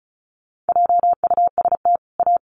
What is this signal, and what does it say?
Morse code, heard as a keyed tone, about 34 wpm, 700 Hz. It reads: JUST A
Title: Sets of 2 Words 35wpm